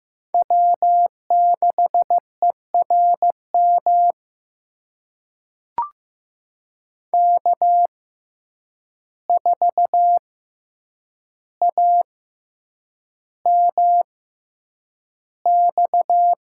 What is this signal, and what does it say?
Morse code, heard as a keyed tone, about 15 wpm, spaced as for 8 wpm, 700 Hz. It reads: W6ERM K4AMX
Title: USA Call Signs - 5 Characters - 6x Character Spacing - 15wpm